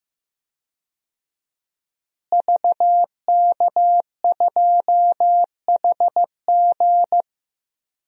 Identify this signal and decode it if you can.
Morse code, heard as a keyed tone, about 15 wpm, 700 Hz. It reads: VK2HG
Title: DXCC-100 Call Signs - 5 Characters - Repeated 3x - 15wpm